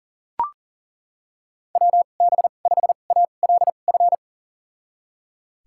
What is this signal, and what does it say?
Morse code, heard as a keyed tone, about 40 wpm, spaced as for 29 wpm, 700 Hz. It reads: WB5ALF
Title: USA Call Signs - 6 Characters - 2x Character Spacing - 40wpm